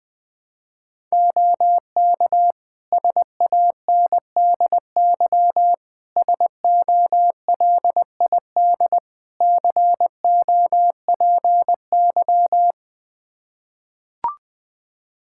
Morse code, heard as a keyed tone, about 20 wpm, 700 Hz. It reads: OK SANDY SOLID COPY
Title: Practice QSO - Exchange 3 - More Info - 20wpm